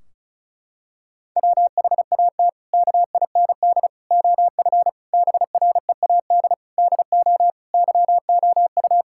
Morse code, heard as a keyed tone, about 36 wpm, 700 Hz. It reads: WHAT KIND OF BREAD DO YOU
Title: Sentences from Top 1600 Words 35wpm